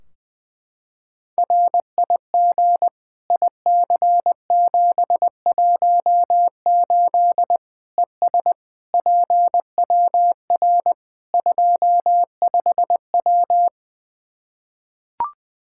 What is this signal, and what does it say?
Morse code, heard as a keyed tone, about 21 wpm, 700 Hz. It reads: RIG IC718 ES PWR 25W
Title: K1USN SST QSO Practice 20wpm